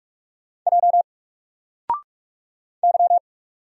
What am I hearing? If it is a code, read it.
Morse code, heard as a keyed tone, about 45 wpm, 700 Hz. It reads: J Y
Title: Single Letter-Number 45wpm